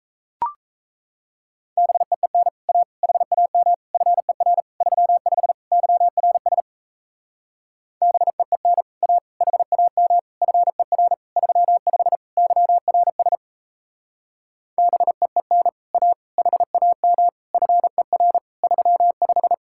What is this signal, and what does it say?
Morse code, heard as a keyed tone, about 36 wpm, 700 Hz. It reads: BEEN A HAM FER 35 YRS BEEN A HAM FER 35 YRS BEEN A HAM FER 35